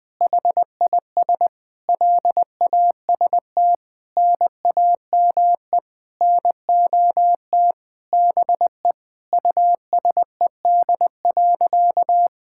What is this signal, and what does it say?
Morse code, heard as a keyed tone, about 20 wpm, 700 Hz. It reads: HIS LAST NAME NOT BE USED.
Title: Sentences with 9-Word Phrases - 20wpm